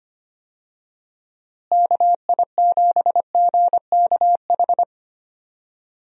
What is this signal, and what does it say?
Morse code, heard as a keyed tone, about 25 wpm, 700 Hz. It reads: KI7GKH
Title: USA Call Signs - 6 Characters - Repeated 2x - 25wpm